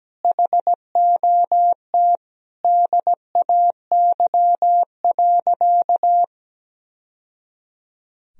Morse code, heard as a keyed tone, about 17 wpm, 700 Hz. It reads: HOT DAY.
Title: Sentences from Top 1000 Words - Review Part 3 - 17wpm